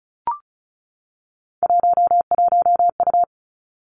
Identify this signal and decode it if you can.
Morse code, heard as a keyed tone, about 36 wpm, 700 Hz. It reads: 11U
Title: Three Challenging Letter-Number Combinations 35wpm